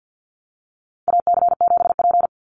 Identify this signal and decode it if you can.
Morse code, heard as a keyed tone, about 49 wpm, 700 Hz. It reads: AC7P
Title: NA SOTA Chaser Callsigns 50wpm